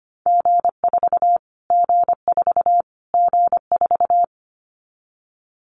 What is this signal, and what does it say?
Morse code, heard as a keyed tone, about 25 wpm, 700 Hz. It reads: G4 G4 G4
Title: Callsign Prefixes for Top 25 DXCC - Repeated 3x - 25wpm